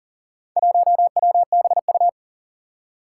Morse code, heard as a keyed tone, about 40 wpm, 700 Hz. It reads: 1WBU
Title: Four Challenging Letter-Number Combinations 40wpm